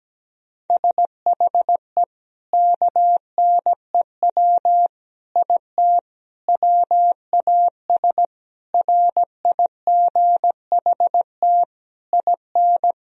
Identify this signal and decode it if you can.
Morse code, heard as a keyed tone, about 17 wpm, 700 Hz. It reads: SHE KNEW IT WAS RIGHT IN
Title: Sentences from Top 1000 Words - Review Part 2 - 17wpm